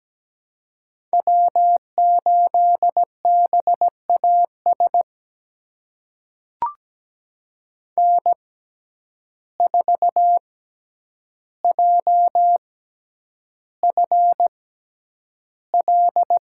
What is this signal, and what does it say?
Morse code, heard as a keyed tone, about 17 wpm, spaced as for 9 wpm, 700 Hz. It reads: W8BAS N4JFL
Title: USA Call Signs - 5 Characters - 6x Character Spacing - 17wpm